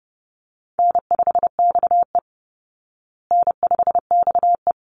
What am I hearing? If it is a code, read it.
Morse code, heard as a keyed tone, about 30 wpm, 700 Hz. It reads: N5XE N5XE
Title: Easy Call Signs Repeated 2x - 30wpm